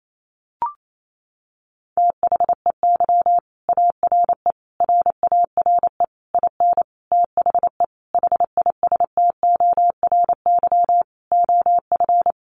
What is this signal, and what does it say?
Morse code, heard as a keyed tone, about 28 wpm, 700 Hz. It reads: THEY ARE RARE IN THE HISTORY OF
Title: Sentences from Top 1100 Words 28wpm